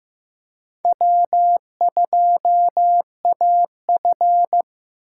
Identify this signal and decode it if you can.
Morse code, heard as a keyed tone, about 15 wpm, 700 Hz. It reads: W2AF
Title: USA Call Signs - 4 Characters - 3x Character Spacing - 15wpm